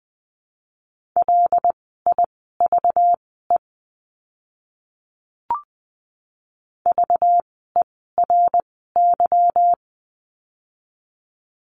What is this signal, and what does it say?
Morse code, heard as a keyed tone, about 20 wpm, spaced as for 14 wpm, 700 Hz. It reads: LIVE VERY
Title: 192 Lesson 31f - Letter v - Words 20wpm